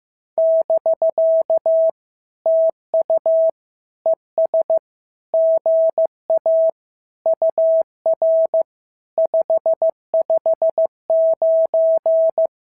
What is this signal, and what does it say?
Morse code, heard as a keyed tone, about 15 wpm, 650 Hz. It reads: <BK> TU ES GA UR 559
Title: SOTA QSO Practice 15wpm